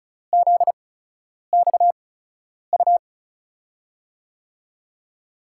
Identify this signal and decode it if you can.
Morse code, heard as a keyed tone, about 36 wpm, spaced as for 10 wpm, 700 Hz. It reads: ZXU
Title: USA Call Signs - 5 Characters - 8x Character Spacing - 35wpm